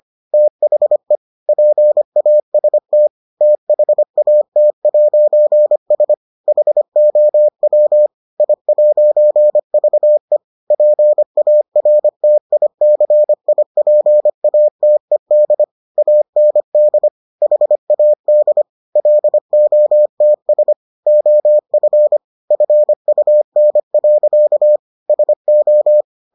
Morse code, heard as a keyed tone, about 25 wpm, 600 Hz. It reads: THE PAST THAT'S HOW I'VE PARTICIPATED AND HAD LOTS OF FUN. SO